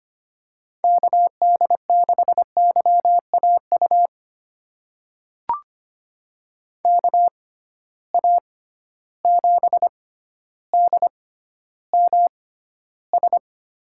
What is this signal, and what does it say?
Morse code, heard as a keyed tone, about 25 wpm, spaced as for 13 wpm, 700 Hz. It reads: KD6YAU KA7DMS
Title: USA Call Signs - 6 Characters - 6x Character Spacing - 25wpm